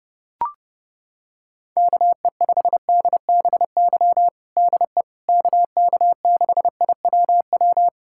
Morse code, heard as a keyed tone, about 30 wpm, 700 Hz. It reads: KE5DBY DE KK6IWW
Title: USA Callsign DE Callsign - 6 Characters - 30wpm